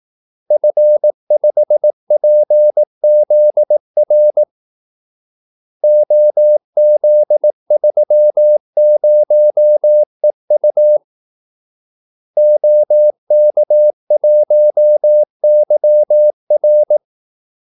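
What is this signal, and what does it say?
Morse code, heard as a keyed tone, about 18 wpm, 600 Hz. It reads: F5PZR OZ30EU OK1YR
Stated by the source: Morse call-sign practice at 18 wpm.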